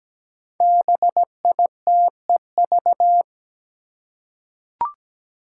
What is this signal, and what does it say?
Morse code, heard as a keyed tone, about 17 wpm, 700 Hz. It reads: BITEV
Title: Five Challenging Letter-Number Combinations 17wpm